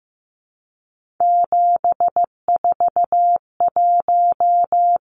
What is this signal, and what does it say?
Morse code, heard as a keyed tone, about 15 wpm, 700 Hz. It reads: 741
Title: Numbers - 3-Digit 15wpm